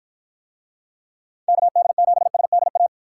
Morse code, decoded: KD7SDA

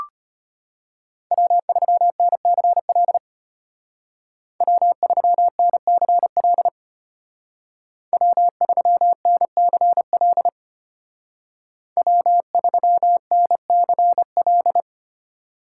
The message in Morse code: W3NCL W3NCL W3NCL W3NCL

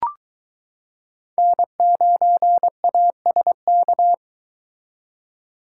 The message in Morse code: N9ASK

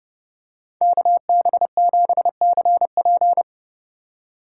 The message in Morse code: KB7CP

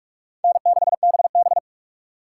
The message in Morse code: N6BB

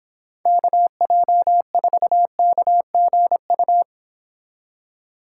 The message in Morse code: KJ4KGU

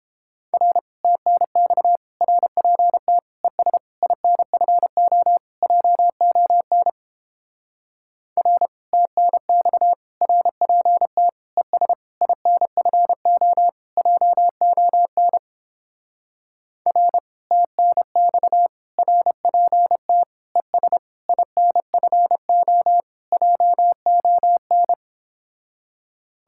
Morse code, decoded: R TNX RPT ES INFO JON R TNX RPT ES INFO JON R TNX RPT ES INFO JON

